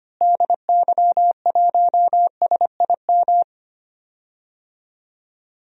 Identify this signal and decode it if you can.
Morse code, heard as a keyed tone, about 25 wpm, 700 Hz. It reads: DY1SIM
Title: DXCC-100 Call Signs - 6 Characters - Repeated 3x - 25wpm